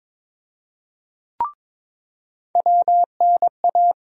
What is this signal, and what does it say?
Morse code, heard as a keyed tone, about 22 wpm, 700 Hz. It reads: WNA
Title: USA Call Sign Postfixes - 3 Letters 22wpm